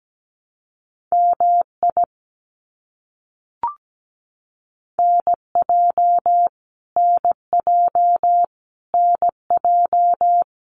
Morse code, heard as a keyed tone, about 17 wpm, 700 Hz. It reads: MI NJ NJ NJ
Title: US States Repeated 3x 17wpm